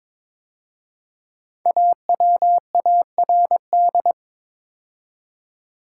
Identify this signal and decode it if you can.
Morse code, heard as a keyed tone, about 22 wpm, 700 Hz. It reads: AWARD
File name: 5 Letter Words - First 3 Letters Given - 22wpm